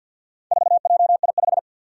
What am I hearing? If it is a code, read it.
Morse code, heard as a keyed tone, about 52 wpm, 700 Hz. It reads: 4JI5
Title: Four Challenging Letter-Number Combinations 50wpm